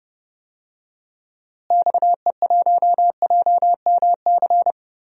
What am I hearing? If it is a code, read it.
Morse code, heard as a keyed tone, about 30 wpm, 700 Hz. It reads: XE1JMC
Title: DXCC-100 Call Signs - 6 Characters - Repeated 2x - 30wpm